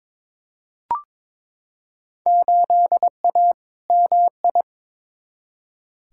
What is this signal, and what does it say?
Morse code, heard as a keyed tone, about 22 wpm, 700 Hz. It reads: 8A MI
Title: Field Day Exchanges Encore - 22wpm